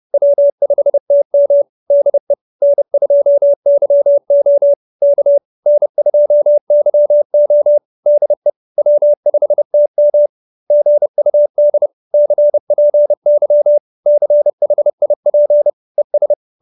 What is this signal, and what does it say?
Morse code, heard as a keyed tone, about 30 wpm, 570 Hz. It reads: W5TM DE N2YO K N2YO DE W5TM GUD CPY CHIP ES
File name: QSO316 30